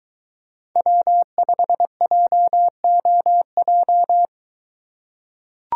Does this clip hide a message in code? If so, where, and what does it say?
Yes: Morse code from 0.8 to 4.3 s, heard as a keyed tone, about 22 wpm, 700 Hz. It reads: W5JOJ